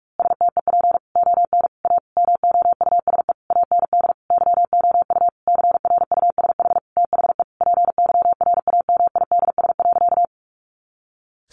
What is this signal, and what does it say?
Morse code, heard as a keyed tone, about 44 wpm, 700 Hz. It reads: STEP ON A MOUSE AND YOU CRUSH THE PYRAMIDS.
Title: A Sound Of Thunder 45wpm